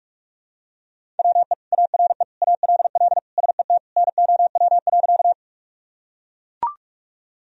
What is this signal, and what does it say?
Morse code, heard as a keyed tone, about 42 wpm, 700 Hz. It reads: WE ARE ALL SET NOW.
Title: Sentences from Top 1000 Words - Review Part 1 - 45wpm